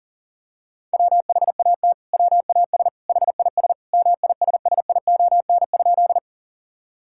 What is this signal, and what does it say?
Morse code, heard as a keyed tone, about 40 wpm, 700 Hz. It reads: WHAT WAS HIS MISSION?
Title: Sentences from Top 1600 Words 40wpm